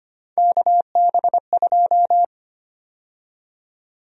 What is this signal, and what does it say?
Morse code, heard as a keyed tone, about 25 wpm, 700 Hz. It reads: KB2
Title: USA Call Sign Prefixes - 25wpm